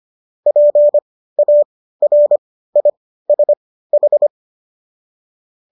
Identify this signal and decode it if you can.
Morse code, heard as a keyed tone, about 25 wpm, spaced as for 15 wpm, 590 Hz. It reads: PARISH